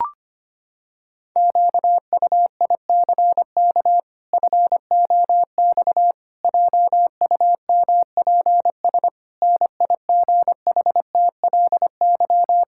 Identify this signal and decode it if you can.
Morse code, heard as a keyed tone, about 25 wpm, 700 Hz. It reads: QUICK FOX JUMPS NIGHTLY
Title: Pangram Challenge v2 - 25wpm